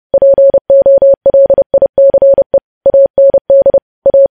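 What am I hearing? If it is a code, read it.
Morse code, heard as a keyed tone, about 30 wpm, 550 Hz. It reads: POLICE AND A
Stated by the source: Morse code practice of news headlines sent at 30 wpm.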